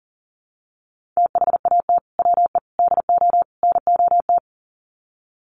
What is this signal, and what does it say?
Morse code, heard as a keyed tone, about 40 wpm, 700 Hz. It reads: THAT WE DO NOT
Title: Sets of 4 Words 40wpm